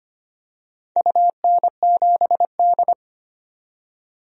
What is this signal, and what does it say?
Morse code, heard as a keyed tone, about 25 wpm, 700 Hz. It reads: UN7D
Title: DXCC-100 Call Signs - 4 Characters - 25wpm